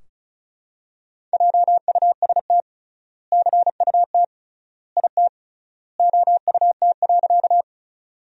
Morse code, heard as a keyed tone, about 34 wpm, 700 Hz. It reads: JUST CUT IT OUT.
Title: 3x Word-Spacing - Sentences from Top 300 Words - 35wpm